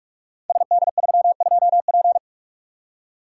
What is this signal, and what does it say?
Morse code, heard as a keyed tone, about 44 wpm, 690 Hz. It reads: SD32P
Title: Code Groups - 5 Letter:Numbers - 45wpm